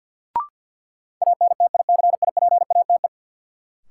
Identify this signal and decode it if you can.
Morse code, heard as a keyed tone, about 51 wpm, 700 Hz. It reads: ANTICIPATE
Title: Sentences from Top 1500 Words - New Word Drill - 50wpm